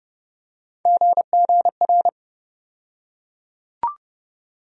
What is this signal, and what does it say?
Morse code, heard as a keyed tone, about 30 wpm, 700 Hz. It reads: GGR
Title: USA Call Sign Postfixes - 3 Letters 30wpm